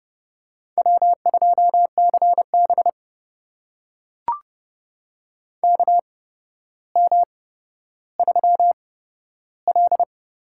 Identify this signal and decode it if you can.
Morse code, heard as a keyed tone, about 30 wpm, spaced as for 13 wpm, 700 Hz. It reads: W2CB KM3L
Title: USA Call Signs - 8x Character Spacing - 30wpm